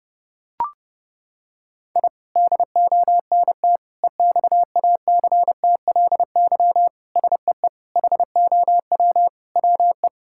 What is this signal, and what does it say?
Morse code, heard as a keyed tone, about 30 wpm, 700 Hz. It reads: I DONT EXACTLY SEE HOW WE
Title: Sentences from Top 1000 Words - Review Part 6 - 30wpm